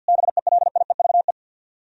Morse code, heard as a keyed tone, about 64 wpm, 700 Hz. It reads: BELIEVE